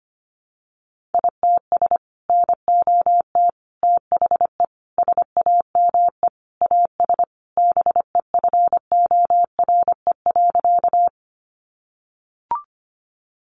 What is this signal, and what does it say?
Morse code, heard as a keyed tone, about 25 wpm, 700 Hz. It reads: ITS NOT THE SAME AS BEFORE.
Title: Sentences from Top 600 Words 25wpm